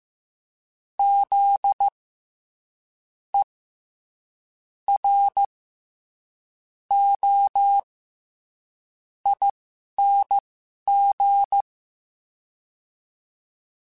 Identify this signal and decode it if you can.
Morse code, heard as a keyed tone, about 15 wpm, spaced as for 11 wpm, 800 Hz. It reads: Z E R O ING
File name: Special INGs-mp3s / Special INGs-mp3s / SlowToFaster ing @10